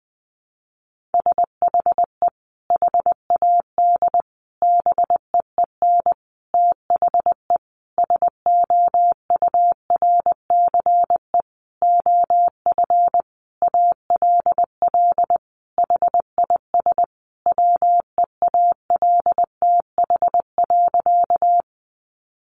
Morse code, heard as a keyed tone, about 20 wpm, 700 Hz. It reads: SHE HAD BEEN THE SOURCE OF ALL HIS WEALTH.
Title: Sentences from Top 1200 Words 20wpm